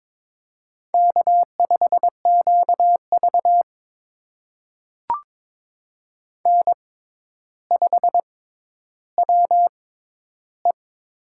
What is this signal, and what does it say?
Morse code, heard as a keyed tone, about 22 wpm, spaced as for 11 wpm, 700 Hz. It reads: K5QV N5WE
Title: USA Call Signs - 6x Character Spacing - 22wpm